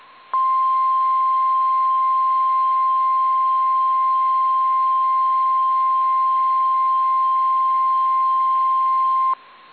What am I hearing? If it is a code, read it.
Emergency Alert System (SAME) tones (the 1050 Hz attention tone). It is heard from 0.3 to 9.4 s.
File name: eas / KSDZKDJL / KSDZKDJL